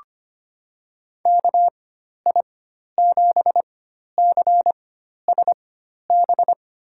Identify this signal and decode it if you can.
Morse code, heard as a keyed tone, about 25 wpm, spaced as for 12 wpm, 700 Hz. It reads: KI7CSB
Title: USA Call Signs - 6 Characters - 4x Character Spacing - 25wpm